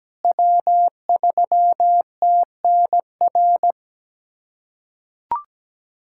Morse code, heard as a keyed tone, about 17 wpm, 700 Hz. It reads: W3TNR
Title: USA Callsign DE Callsign - 5 Characters - 17wpm